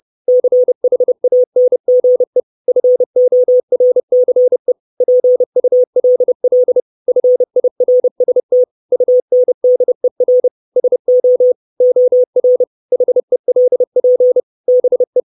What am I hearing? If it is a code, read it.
Morse code, heard as a keyed tone, about 30 wpm, 500 Hz. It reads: CHANGE FORCE PULL FIRST UNDER SO OR HELP BE
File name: 632 Words 30WPM